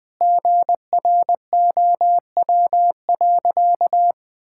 Morse code, heard as a keyed tone, about 19 wpm, 700 Hz. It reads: GROW.
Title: Sentences with 9-Word Phrases - 20wpm